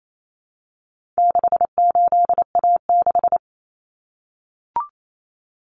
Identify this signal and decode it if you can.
Morse code, heard as a keyed tone, about 28 wpm, 700 Hz. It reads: 68A6
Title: Common Cut Numbers - 4-Digit 28wpm